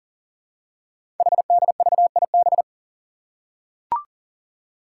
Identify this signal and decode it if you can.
Morse code, heard as a keyed tone, about 40 wpm, 700 Hz. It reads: HDVIB
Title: Code Groups - 5 Letters - 40wpm